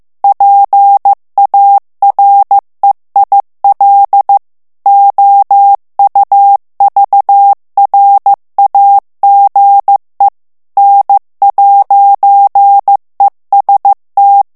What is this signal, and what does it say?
Morse code, heard as a keyed tone, about 15 wpm, 800 Hz. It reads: PAREIL OUVRAGE N'EST
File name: morse mp3 / FRANCAIS TEXTE CLAIR 2 VITESSE 14 MOTS MINUTES